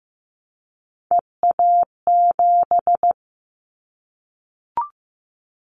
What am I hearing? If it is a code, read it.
Morse code, heard as a keyed tone, about 15 wpm, 700 Hz. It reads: EA7
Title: Callsign Prefixes for Top 25 DXCC - Repeated 3x - 15wpm